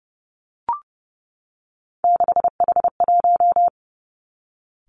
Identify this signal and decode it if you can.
Morse code, heard as a keyed tone, about 32 wpm, 700 Hz. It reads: 6H1
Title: Three Challenging Letter-Number Combinations 30wpm